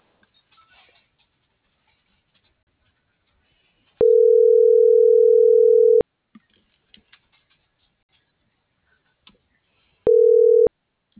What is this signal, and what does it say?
Ringback tone (North America), 440 + 480 Hz.
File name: CENTRAL BLN / out ximena.diaz